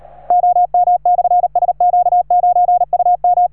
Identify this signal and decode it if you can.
Morse code, heard as a keyed tone, about 40 wpm, 710 Hz. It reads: OM/SQ9UM
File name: RS3A-20